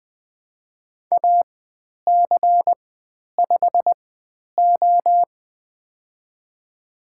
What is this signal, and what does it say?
Morse code, heard as a keyed tone, about 20 wpm, spaced as for 10 wpm, 700 Hz. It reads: AC5O
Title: 109 Lesson 18g - Letter m - Call Signs 20wpm